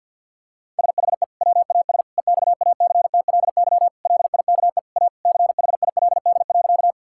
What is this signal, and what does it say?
Morse code, heard as a keyed tone, about 50 wpm, 700 Hz. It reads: SHE WAS EXACTLY LIKE A CHILD.
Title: Sentences from Top 1000 Words - Review Part 6 - 50wpm